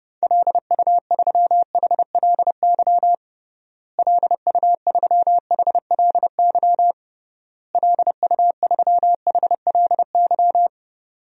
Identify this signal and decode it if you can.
Morse code, heard as a keyed tone, about 30 wpm, 700 Hz. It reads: LU3HLY LU3HLY LU3HLY
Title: DXCC-100 Call Signs - 6 Characters - Repeated 3x - 30wpm